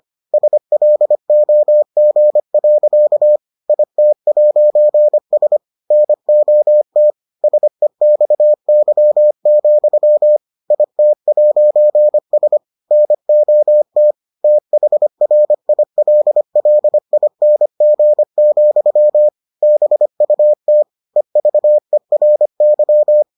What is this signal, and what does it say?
Morse code, heard as a keyed tone, about 25 wpm, 600 Hz. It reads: SLOG. IT'S NOT SEXY, IT'S NOT THRILLING, BUT EVERY